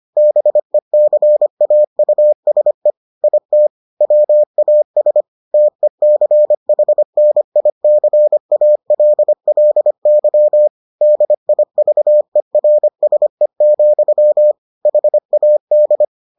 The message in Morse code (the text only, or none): BECAUSE IT WAS TECHNICALLY DIVERSE, HAD